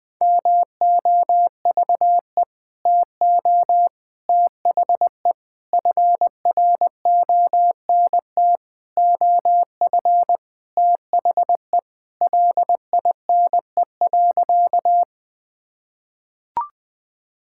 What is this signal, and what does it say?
Morse code, heard as a keyed tone, about 20 wpm, 700 Hz. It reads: MOVE TO THE FRONT OF THE LINE.
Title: Sentences with 5-Word Phrases - 20wpm